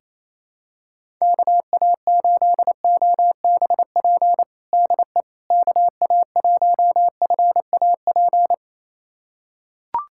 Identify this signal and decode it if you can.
Morse code, heard as a keyed tone, about 27 wpm, 700 Hz. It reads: KA8OBP DE KA1FAP
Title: USA Callsign DE Callsign - 6 Characters - 28wpm